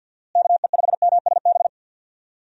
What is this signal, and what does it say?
Morse code, heard as a keyed tone, about 48 wpm, 700 Hz. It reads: KE5MSB
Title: USA Call Signs - 6 Characters - Repeated 3x - 50wpm